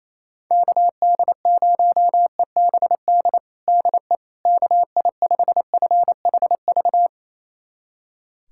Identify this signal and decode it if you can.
Morse code, heard as a keyed tone, about 28 wpm, 700 Hz. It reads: KD0EBD DE KI5FHV
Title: USA Callsign DE Callsign - 6 Characters - 28wpm